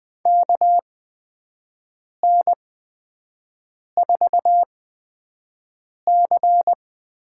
Morse code, heard as a keyed tone, about 20 wpm, spaced as for 5 wpm, 700 Hz. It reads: KN4C
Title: USA Call Signs - 6 Characters - 8x Character Spacing - 20wpm